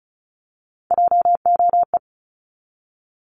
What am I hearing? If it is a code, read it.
Morse code, heard as a keyed tone, about 36 wpm, 700 Hz. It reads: JOE
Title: Multi-letter Phonemes 35wpm